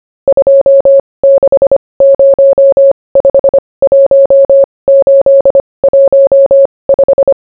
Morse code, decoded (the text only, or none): 26051815